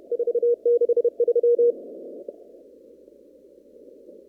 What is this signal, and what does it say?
Morse code, heard as a keyed tone, about 32 wpm, 480 Hz. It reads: 463
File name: CW / kt5j wpx cw-op-k5pi